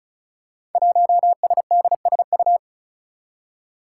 Morse code, heard as a keyed tone, about 36 wpm, 700 Hz. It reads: 1SDSU
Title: Five Challenging Letter-Number Combinations 35wpm